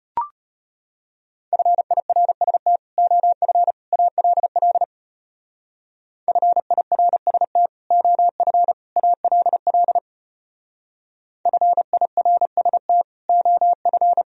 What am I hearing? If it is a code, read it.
Morse code, heard as a keyed tone, about 31 wpm, 700 Hz. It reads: FIRST OF ALL FIRST OF ALL FIRST OF
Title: Speed-Racing - Sets of 3 Words 25wpm